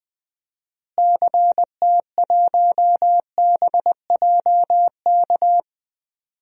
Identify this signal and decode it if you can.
Morse code, heard as a keyed tone, about 20 wpm, 700 Hz. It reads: CT1BJK
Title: DXCC-100 Call Signs - 6 Characters - Repeated 2x - 20wpm